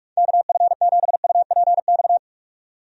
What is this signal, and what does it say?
Morse code, heard as a keyed tone, about 46 wpm, 700 Hz. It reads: KF7UPX
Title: USA Call Signs - 6 Characters - 45wpm